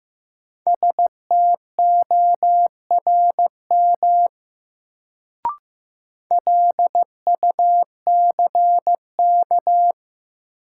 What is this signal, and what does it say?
Morse code, heard as a keyed tone, about 15 wpm, 700 Hz. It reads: STORM LUCK
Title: Sentences from Top 1500 Words - New Word Drill Alt - 15wpm